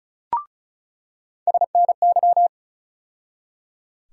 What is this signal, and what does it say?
Morse code, heard as a keyed tone, about 34 wpm, 700 Hz. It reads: SNY
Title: USA Call Sign Postfixes - 3 Letters 35wpm